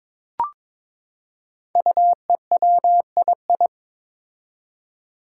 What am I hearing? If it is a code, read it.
Morse code, heard as a keyed tone, about 22 wpm, 700 Hz. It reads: UEWII
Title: Five Challenging Letter-Number Combinations 22wpm